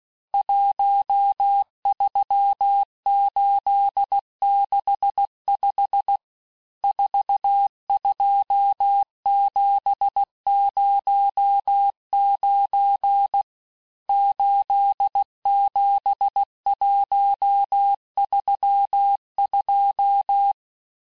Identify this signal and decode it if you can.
Morse code, heard as a keyed tone, about 16 wpm, 800 Hz. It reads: 13865 42709 87132
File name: cours-du-lundi files / Cours-1586-texte-chiffre-vitesse-15mpm